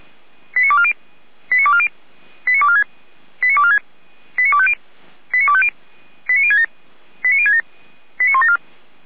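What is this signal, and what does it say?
Five-tone selective call (ZVEI), heard as quick runs of five tones, tones of about 70 ms. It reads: 89240 89240 89246 89246 89250 89250 89067 89067 89174
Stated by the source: ZVEI five-tone call 89160 received on an SDR, recorded by Feuerwehr Orlamünde.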